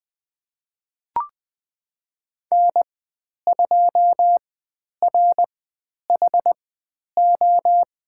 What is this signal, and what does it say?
Morse code, heard as a keyed tone, about 20 wpm, spaced as for 10 wpm, 700 Hz. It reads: N2RHO
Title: 102 Lesson 17g - Letter u - Call Signs 20wpm